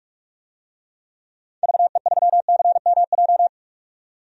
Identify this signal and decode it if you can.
Morse code, heard as a keyed tone, about 46 wpm, 700 Hz. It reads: VE3CMJ
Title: DXCC-100 Call Signs - 6 Characters - 45wpm